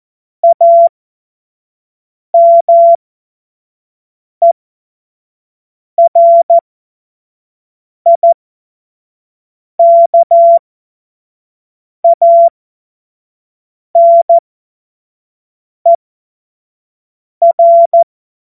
Morse code, heard as a keyed tone, about 14 wpm, spaced as for 5 wpm, 680 Hz. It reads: AMERIKANER
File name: Morsekurs / Witze / Witz 01. März 2023 25 F70